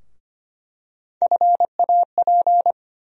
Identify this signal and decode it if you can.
Morse code, heard as a keyed tone, about 25 wpm, 700 Hz. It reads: FAP
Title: USA Call Sign Postfixes - 3 Letters 25wpm